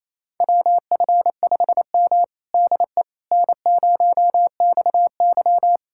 Morse code, heard as a keyed tone, about 28 wpm, 700 Hz. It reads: WF5M DE N0XY
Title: USA Callsign DE Callsign - 4 Characters - 28wpm